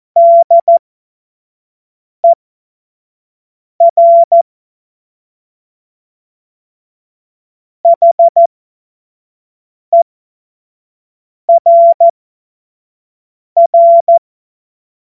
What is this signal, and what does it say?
Morse code, heard as a keyed tone, about 14 wpm, spaced as for 5 wpm, 680 Hz. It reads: DER HERR
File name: Morsekurs / Witze / Witz 04. Juli 2023 25 F70